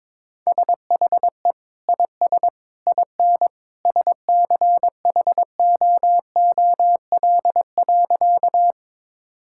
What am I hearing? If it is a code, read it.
Morse code, heard as a keyed tone, about 22 wpm, 700 Hz. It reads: SHE IS IN SCHOOL.
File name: practice / 22wpm / Sentences with 2-Word Phrases - 22wpm